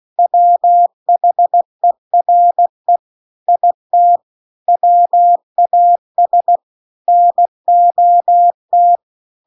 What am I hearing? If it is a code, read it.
Morse code, heard as a keyed tone, about 16 wpm, 700 Hz. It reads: WHERE IT WAS NOT
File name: CW-The-Big-Twoheaded-River-2-16WPM